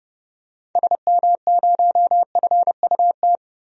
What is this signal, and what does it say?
Morse code, heard as a keyed tone, about 30 wpm, 700 Hz. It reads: SM0FUT
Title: DXCC-100 Call Signs - 6 Characters - Repeated 2x - 30wpm